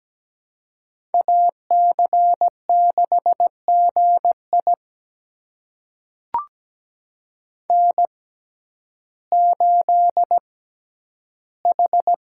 Morse code, heard as a keyed tone, about 17 wpm, spaced as for 10 wpm, 700 Hz. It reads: AC6GI N8H
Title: USA Call Signs - 5 Characters - 6x Character Spacing - 17wpm